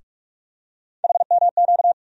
Morse code, heard as a keyed tone, about 43 wpm, 700 Hz. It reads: HMQ
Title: USA Call Sign Postfixes - 3 Letters 45wpm